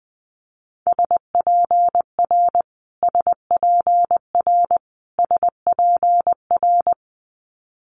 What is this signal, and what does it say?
Morse code, heard as a keyed tone, about 19 wpm, 700 Hz. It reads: SPR SPR SPR
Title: Multi-letter Phonemes Repeated 3x 20wpm